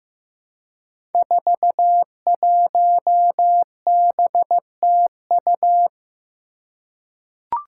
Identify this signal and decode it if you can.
Morse code, heard as a keyed tone, about 15 wpm, 700 Hz. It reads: 41BTU
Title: Five Challenging Letter-Number Combinations 15wpm